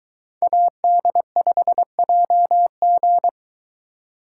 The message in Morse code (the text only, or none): AD5JG